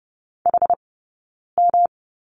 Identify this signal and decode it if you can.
Morse code, heard as a keyed tone, about 30 wpm, spaced as for 9 wpm, 690 Hz. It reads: HM
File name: Lesson 18d - Letter m - Two Characters 30wpm